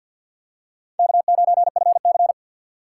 Morse code, heard as a keyed tone, about 51 wpm, 700 Hz. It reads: K9FC
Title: USA Call Signs - 4 Characters - 2x Character Spacing - 50wpm